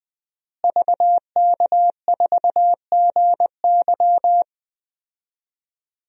Morse code, heard as a keyed tone, about 21 wpm, 700 Hz. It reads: VK4GY
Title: DXCC-100 Call Signs - 5 Characters - Repeated 2x - 20wpm